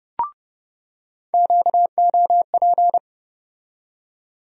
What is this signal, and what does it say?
Morse code, heard as a keyed tone, about 30 wpm, 700 Hz. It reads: QOP
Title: USA Call Sign Postfixes - 3 Letters 30wpm